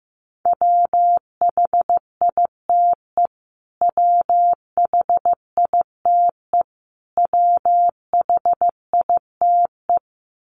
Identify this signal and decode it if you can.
Morse code, heard as a keyed tone, about 15 wpm, 700 Hz. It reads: WHITE WHITE WHITE
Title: Top 300 Words Repeated 15wpm